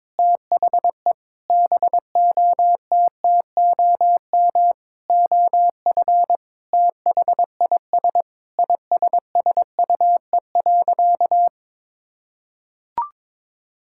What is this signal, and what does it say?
Morse code, heard as a keyed tone, about 22 wpm, 700 Hz. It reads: THE BOTTOM OF THIS ISSUE.
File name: Sentences with 7-Word Phrases - 22wpm